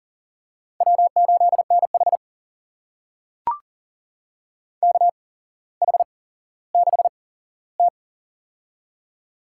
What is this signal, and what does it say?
Morse code, heard as a keyed tone, about 42 wpm, spaced as for 17 wpm, 700 Hz. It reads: W8NH KH6T
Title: USA Call Signs - 4 Characters - 8x Character Spacing - 40wpm